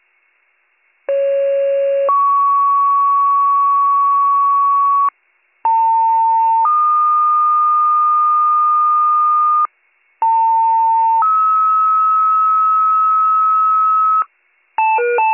Two-tone page, heard as a one-second tone, then a three-second tone, about 550 then 1120 Hz, about 900 then 1220 Hz, about 900 then 1290 Hz. It is heard from 1.1 to 14.2 s.